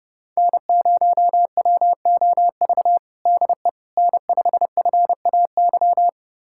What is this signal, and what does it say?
Morse code, heard as a keyed tone, about 30 wpm, 700 Hz. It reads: N0WOV DE N5FAY
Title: USA Callsign DE Callsign - 5 Characters - 30wpm